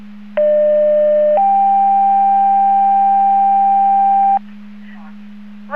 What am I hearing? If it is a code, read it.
Two-tone page, heard as a one-second tone, then a three-second tone, about 600 then 790 Hz. It is heard from 0.4 to 4.4 s.